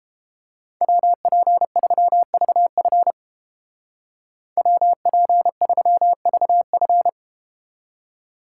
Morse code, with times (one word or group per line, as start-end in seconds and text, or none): 0.8-3.1 WP3VF
4.6-7.1 WP3VF